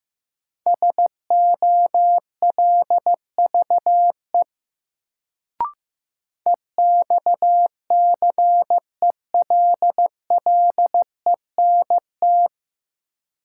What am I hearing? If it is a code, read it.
Morse code, heard as a keyed tone, about 15 wpm, 700 Hz. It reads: SOLVE EXCELLENT
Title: Sentences from Top 1000 Words - New Word Drill - 15wpm